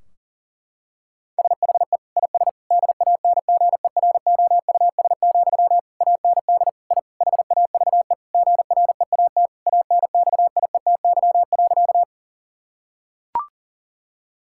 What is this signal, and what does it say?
Morse code, heard as a keyed tone, about 40 wpm, 700 Hz. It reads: SHE IS DANGEROUS, AND I HAVE GREAT ANXIETY.
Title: Sentences from Top 1600 Words 40wpm